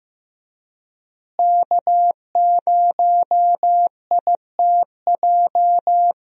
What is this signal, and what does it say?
Morse code, heard as a keyed tone, about 15 wpm, 700 Hz. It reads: K0ITJ
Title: USA Call Signs - 5 Characters - Repeated 3x - 15wpm